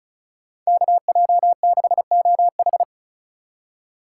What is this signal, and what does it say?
Morse code, heard as a keyed tone, about 43 wpm, 700 Hz. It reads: KJ6OH
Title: Speed Racing - USA 2x2 Call Signs 35wpm